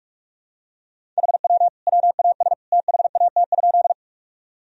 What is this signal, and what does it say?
Morse code, heard as a keyed tone, about 42 wpm, 700 Hz. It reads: HW WAS THAT?
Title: Sentences from Top 100 Words and Abbreviations 45wpm